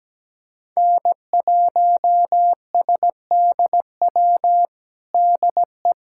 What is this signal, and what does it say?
Morse code, heard as a keyed tone, about 17 wpm, 700 Hz. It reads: N1SDW DE
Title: USA Callsign DE Callsign - 5 Characters - 17wpm